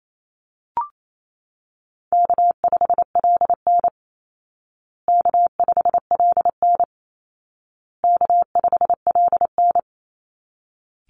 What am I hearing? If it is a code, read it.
Morse code, heard as a keyed tone, about 28 wpm, 700 Hz. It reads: K5LN K5LN K5LN
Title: Easy Call Signs Repeated 3x - 28wpm